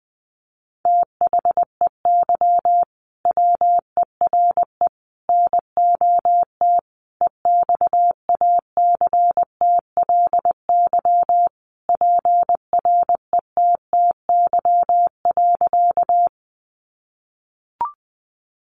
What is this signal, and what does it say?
Morse code, heard as a keyed tone, about 19 wpm, 700 Hz. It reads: THEY WERE NOT EXACTLY PRETTY.
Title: Sentences from Top 1000 Words 20wpm